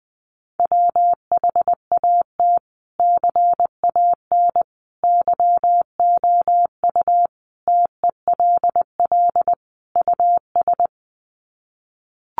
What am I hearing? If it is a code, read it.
Morse code, heard as a keyed tone, about 20 wpm, 700 Hz. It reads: WHAT CAN YOU TELL US
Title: Sets of 5 Words Encore 20wpm